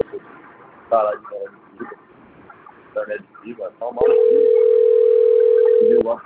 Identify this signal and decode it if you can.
Ringback tone (North America), 440 + 480 Hz, with speech.
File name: rg-1600-unknown- - - .76873